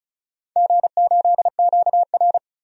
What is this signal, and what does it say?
Morse code, heard as a keyed tone, about 34 wpm, 700 Hz. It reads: G8QR
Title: DXCC-100 Call Signs - 4 Characters - 35wpm